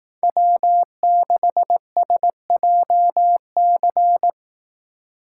Morse code, heard as a keyed tone, about 18 wpm, 700 Hz. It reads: W6SJC
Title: Speed Racing - USA Call Signs 15wpm